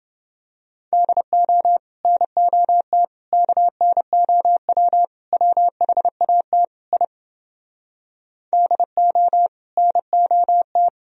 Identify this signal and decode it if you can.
Morse code, heard as a keyed tone, about 30 wpm, 700 Hz. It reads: DO NOT KNOW WHAT I DO NOT